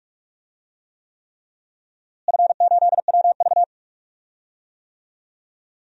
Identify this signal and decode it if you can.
Morse code, heard as a keyed tone, about 42 wpm, 700 Hz. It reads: F8WV
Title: DXCC-100 Call Signs - 4 Characters - Repeated 3x - 45wpm